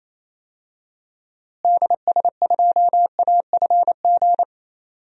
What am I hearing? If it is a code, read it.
Morse code, heard as a keyed tone, about 28 wpm, 700 Hz. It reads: DS2AFG
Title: DXCC-100 Call Signs - 6 Characters - Repeated 2x - 28wpm